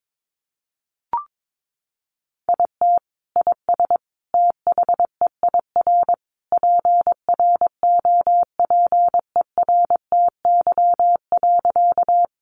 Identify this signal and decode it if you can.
Morse code, heard as a keyed tone, about 22 wpm, 700 Hz. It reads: IT IS THEIR PROPERTY.
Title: Sentences from Top 1000 Words 22wpm